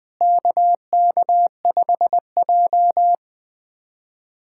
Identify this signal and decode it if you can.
Morse code, heard as a keyed tone, about 20 wpm, 700 Hz. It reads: KK5J
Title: USA Callsign DE Callsign - 4 Characters - 20wpm